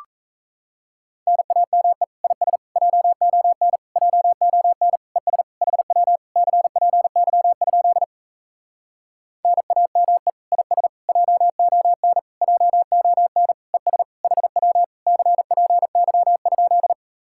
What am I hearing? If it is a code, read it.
Morse code, heard as a keyed tone, about 37 wpm, 700 Hz. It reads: NAME IS JON JON ES HW CPY? NAME IS JON JON ES HW CPY?